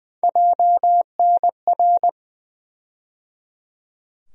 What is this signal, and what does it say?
Morse code, heard as a keyed tone, about 20 wpm, 700 Hz. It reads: JNR
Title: USA Call Sign Postfixes - 3 Letters 20wpm